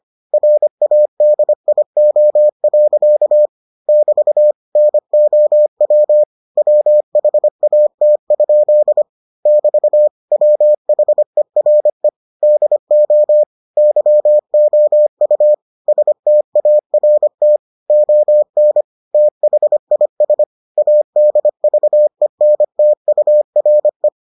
Morse code, heard as a keyed tone, about 25 wpm, 600 Hz. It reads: RADIO. <BT> NOW WHAT? <BT> WHERE DO YOU START ON THIS ADVENTURE